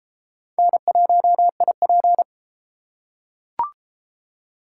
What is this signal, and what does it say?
Morse code, heard as a keyed tone, about 28 wpm, 700 Hz. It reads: N1IP